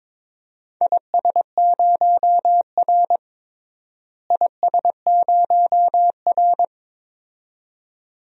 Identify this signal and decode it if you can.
Morse code, heard as a keyed tone, about 22 wpm, 700 Hz. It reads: IS0R IS0R
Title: DXCC-100 Call Signs - 4 Characters - Repeated 2x - 22wpm